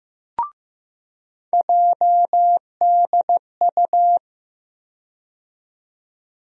Morse code, heard as a keyed tone, about 15 wpm, 700 Hz. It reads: JDU
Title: USA Call Sign Postfixes - 3 Letters 15wpm